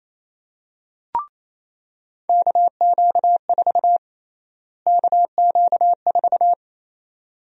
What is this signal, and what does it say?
Morse code, heard as a keyed tone, about 28 wpm, 700 Hz. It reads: KQ4 KQ4
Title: USA Call Sign Prefixes - Repeated 2x - 28wpm